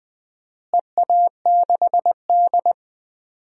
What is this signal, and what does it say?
Morse code, heard as a keyed tone, about 20 wpm, 700 Hz. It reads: EA6D
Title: DXCC-100 Call Signs - 4 Characters - Repeated 3x - 20wpm